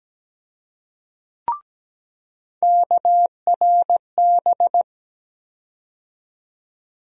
Morse code, heard as a keyed tone, about 17 wpm, 700 Hz. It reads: KRB
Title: USA Call Sign Postfixes - 3 Letters 17wpm